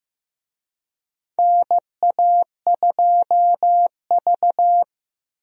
Morse code, heard as a keyed tone, about 15 wpm, 700 Hz. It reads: NA2V
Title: USA Call Signs - 4 Characters - Repeated 3x - 15wpm